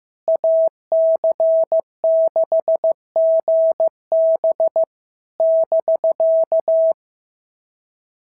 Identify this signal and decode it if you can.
Morse code, heard as a keyed tone, about 15 wpm, 650 Hz. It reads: AC6GB <BK>
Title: SOTA QSO Practice 15wpm